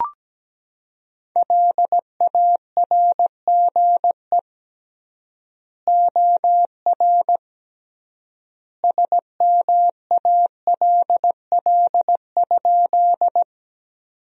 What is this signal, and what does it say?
Morse code, heard as a keyed tone, about 17 wpm, 700 Hz. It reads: LARGE OR SMALL?
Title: 3x Word-Spacing - Sentences from Top 200 Words - 17wpm